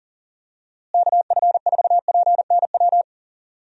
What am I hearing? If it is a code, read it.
Morse code, heard as a keyed tone, about 40 wpm, 700 Hz. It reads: KF4PNW
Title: USA Call Signs - 6 Characters - Repeated 2x - 40wpm